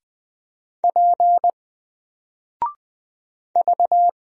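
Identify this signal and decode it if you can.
Morse code, heard as a keyed tone, about 20 wpm, 700 Hz. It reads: P V
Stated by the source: Single letter and number Morse drill at 20 wpm, from N4LCD.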